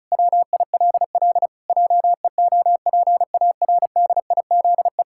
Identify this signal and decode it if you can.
Morse code, heard as a keyed tone, about 36 wpm, 700 Hz. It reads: WILL JEOPARDIZE
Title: Pangram Challenge v2 - 35wpm